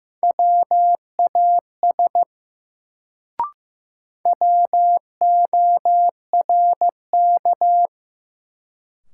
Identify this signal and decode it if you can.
Morse code, heard as a keyed tone, about 15 wpm, 700 Hz. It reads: WAS WORK
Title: Sets of 2 Words - New Word Drill - 15wpm